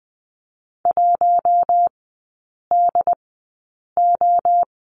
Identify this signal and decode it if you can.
Morse code, heard as a keyed tone, about 20 wpm, spaced as for 8 wpm, 700 Hz. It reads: 1DO
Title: 067 Lesson 12e - Letter d - Three Characters 20wpm